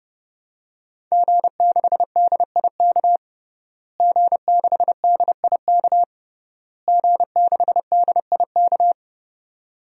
Morse code, heard as a keyed tone, about 32 wpm, 700 Hz. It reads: G6DIK G6DIK G6DIK
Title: DXCC-100 Call Signs - 5 Characters - Repeated 3x - 30wpm